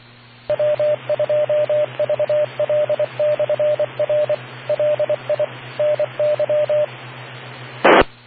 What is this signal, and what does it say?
Morse code, heard as a keyed tone, about 24 wpm, 600 Hz, with radio noise. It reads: W2VL/R LI NY I